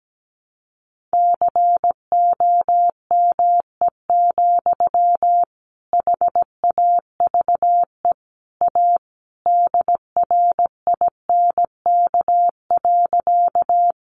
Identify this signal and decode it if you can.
Morse code, heard as a keyed tone, about 17 wpm, 700 Hz. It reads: COME, HAVE A DRINK.
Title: Sentences from Top 1500 Words 17wpm